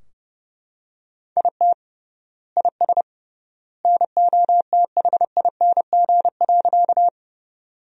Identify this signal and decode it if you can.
Morse code, heard as a keyed tone, about 30 wpm, 700 Hz. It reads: IT IS NOTHING.
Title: 3x Word-Spacing - Sentences from Top 500 Words - 30wpm